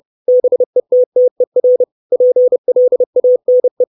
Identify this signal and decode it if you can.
Morse code, heard as a keyed tone, about 30 wpm, 500 Hz. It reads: BETTER PLANE
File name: 632 Words 30WPM